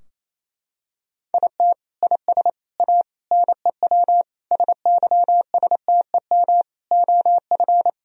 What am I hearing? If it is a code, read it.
Morse code, heard as a keyed tone, about 28 wpm, 700 Hz. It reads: IT IS A NEW SYSTEM OF
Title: Sentences from Top 1000 Words - Review Part 3 - 28wpm